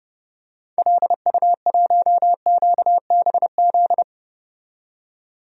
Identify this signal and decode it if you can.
Morse code, heard as a keyed tone, about 31 wpm, 700 Hz. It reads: LU1QBZ
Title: DXCC-100 Call Signs - 6 Characters - 30wpm